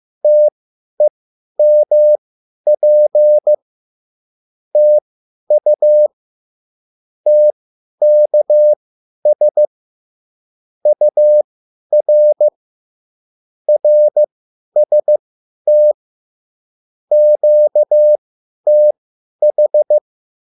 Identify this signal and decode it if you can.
Morse code, heard as a keyed tone, about 15 wpm, spaced as for 10 wpm, 600 Hz. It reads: TEMP TU TKS UR RST QTH
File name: QSL / CW lesson